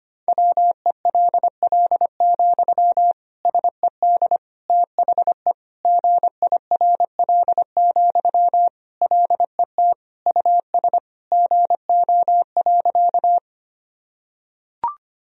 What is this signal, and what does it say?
Morse code, heard as a keyed tone, about 25 wpm, 700 Hz. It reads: WELL, SED THE GIRL, LET US GO.
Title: Sentences from Top 400 Words and Abbreviations 25wpm